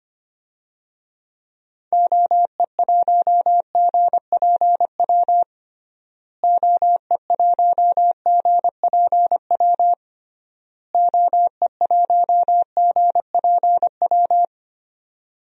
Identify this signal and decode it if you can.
Morse code, heard as a keyed tone, about 25 wpm, 700 Hz. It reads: OE1GPW OE1GPW OE1GPW
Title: DXCC-100 Call Signs - 6 Characters - Repeated 3x - 25wpm